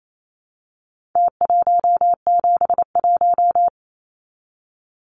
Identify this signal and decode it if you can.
Morse code, heard as a keyed tone, about 28 wpm, 700 Hz. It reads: T171
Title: Numbers - Contest Style - 4-Digit 28wpm